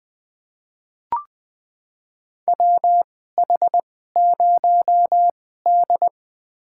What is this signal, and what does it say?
Morse code, heard as a keyed tone, about 20 wpm, spaced as for 14 wpm, 700 Hz. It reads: WH0D
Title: USA Call Signs - 4 Characters - 2x Character Spacing - 20wpm